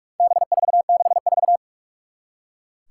Morse code, heard as a keyed tone, about 44 wpm, 700 Hz. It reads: B464
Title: Four Challenging Letter-Number Combinations 45wpm